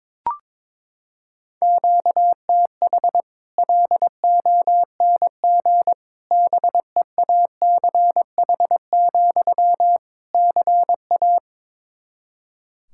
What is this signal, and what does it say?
Morse code, heard as a keyed tone, about 22 wpm, 700 Hz. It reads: QTH LONG BEACH, CA
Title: Practice QSO - Exchange 1 - Beginning - 22wpm